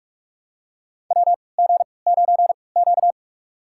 Morse code, heard as a keyed tone, about 45 wpm, spaced as for 25 wpm, 700 Hz. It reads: WG9Q
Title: USA Call Signs - 4 Characters - 3x Character Spacing - 45wpm